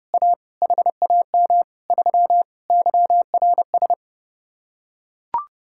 Morse code, heard as a keyed tone, about 30 wpm, 700 Hz. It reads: A HAM 3 YRS